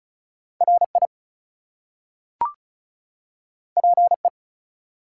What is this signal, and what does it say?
Morse code, heard as a keyed tone, about 36 wpm, 700 Hz. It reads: RI PE
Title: US States and Canadian Provinces 35wpm